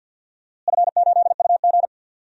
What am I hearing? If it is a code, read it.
Morse code, heard as a keyed tone, about 51 wpm, 700 Hz. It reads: F8UG
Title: DXCC-100 Call Signs - 4 Characters - Repeated 2x - 50wpm